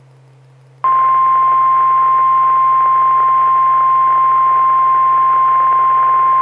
Emergency Alert System (SAME) tones (the 1050 Hz attention tone). It is heard from 0.9 to 5.8 s.
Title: Saline, AK SKYWARN Repeater System - W5RHS